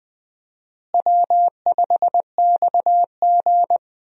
Morse code, heard as a keyed tone, about 20 wpm, 700 Hz. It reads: W5XG
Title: USA Call Signs - 4 Characters - Repeated 2x - 20wpm